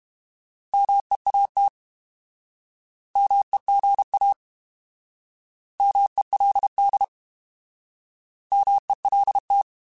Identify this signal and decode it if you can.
Morse code, heard as a keyed tone, about 32 wpm, 800 Hz. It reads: MEAT MEGA MELD MELT
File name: Alphabetical / Alphabetical mp3s / Ms @30